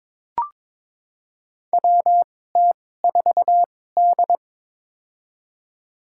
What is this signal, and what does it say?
Morse code, heard as a keyed tone, about 22 wpm, spaced as for 16 wpm, 700 Hz. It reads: WT4D
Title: USA Call Signs - 4 Characters - 2x Character Spacing - 22wpm